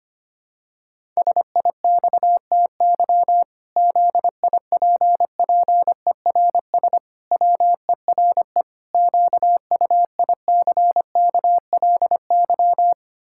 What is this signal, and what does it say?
Morse code, heard as a keyed tone, about 25 wpm, 700 Hz. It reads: SIXTY ZIPPERS WERE QUICKLY
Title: Pangram Challenge v2 - 25wpm